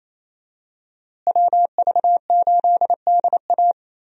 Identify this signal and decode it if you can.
Morse code, heard as a keyed tone, about 27 wpm, 700 Hz. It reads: WV8DA
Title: USA Call Signs - 5 Characters - Repeated 3x - 28wpm